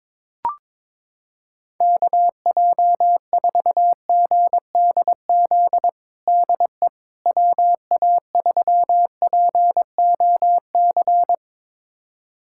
Morse code, heard as a keyed tone, about 22 wpm, 700 Hz. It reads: KJ4GDZ DE WA3POC
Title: USA Callsign DE Callsign 22wpm